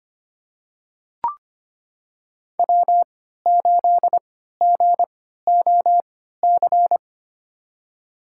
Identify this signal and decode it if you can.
Morse code, heard as a keyed tone, about 25 wpm, spaced as for 14 wpm, 700 Hz. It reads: W8GOC
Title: USA Call Signs - 5 Characters - 3x Character Spacing - 25wpm